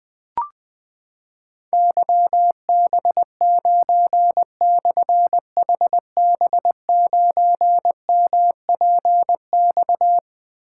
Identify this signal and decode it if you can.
Morse code, heard as a keyed tone, about 21 wpm, 700 Hz. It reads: YB9/HB9MPX
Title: Difficult Call Signs 20wpm